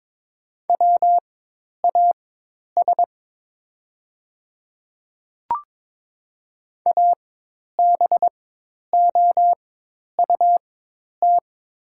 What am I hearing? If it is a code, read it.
Morse code, heard as a keyed tone, about 22 wpm, spaced as for 10 wpm, 700 Hz. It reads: WAS ABOUT
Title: 4x Character Spacing - Top 100 Words - 22wpm